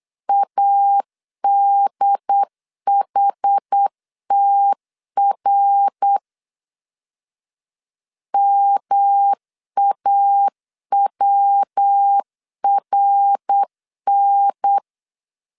Morse code, heard as a keyed tone, about 8 wpm, 800 Hz. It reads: ADHTR MAWRN
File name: CW / DET / 1 3 1500C S A 8wpm 4229